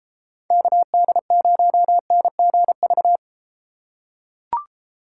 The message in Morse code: KD0NGV

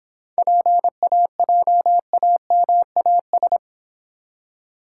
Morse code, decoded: PAJAMAS

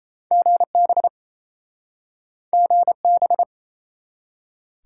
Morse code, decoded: GB GB

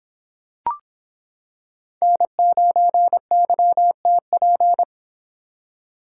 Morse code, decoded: N9YTP

